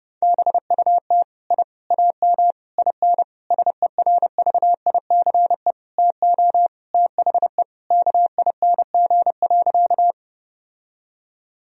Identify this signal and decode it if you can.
Morse code, heard as a keyed tone, about 30 wpm, 700 Hz. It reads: BUT I AM IN SERVICE TO THE KING.
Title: Sentences from Top 1000 Words - Review Part 4 - 30wpm